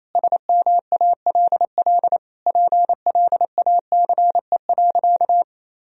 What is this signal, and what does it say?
Morse code, heard as a keyed tone, about 28 wpm, 700 Hz. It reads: SMALL PLACE.
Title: Sentences from Top 1000 Words - Review Part 1 - 28wpm